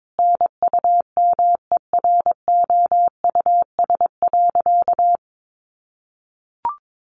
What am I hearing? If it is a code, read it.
Morse code, heard as a keyed tone, about 22 wpm, 700 Hz. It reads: NUMEROUS.
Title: Sentences from Top 1400 Words 22wpm